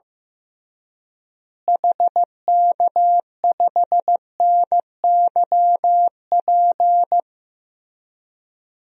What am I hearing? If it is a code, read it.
Morse code, heard as a keyed tone, about 15 wpm, 700 Hz. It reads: HK5NYP
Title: DXCC-100 Call Signs - 6 Characters - Repeated 2x - 15wpm